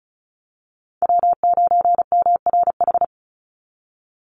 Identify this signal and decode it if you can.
Morse code, heard as a keyed tone, about 34 wpm, 700 Hz. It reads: W9MRH
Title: NA SOTA Chaser Callsigns 35wpm